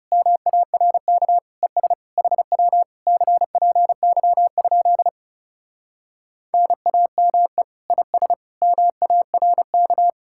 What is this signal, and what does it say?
Morse code, heard as a keyed tone, about 40 wpm, 700 Hz. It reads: MARK ES HW CPY? NAME IS MARK